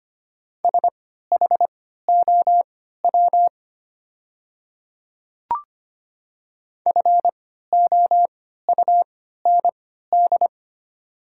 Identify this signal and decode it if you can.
Morse code, heard as a keyed tone, about 25 wpm, spaced as for 14 wpm, 690 Hz. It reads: SHOW FOUND
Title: 3x Character Spacing - Top 300 Words - 25wpm